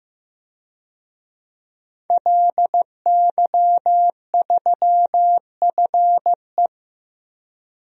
Morse code, heard as a keyed tone, about 16 wpm, 700 Hz. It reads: LY3FE
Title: DXCC-100 Call Signs - 5 Characters - Repeated 2x - 15wpm